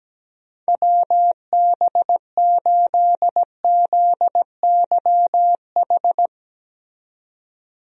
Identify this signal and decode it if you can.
Morse code, heard as a keyed tone, about 17 wpm, 700 Hz. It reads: WB8ZYH
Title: USA Call Signs - 6 Characters - 17wpm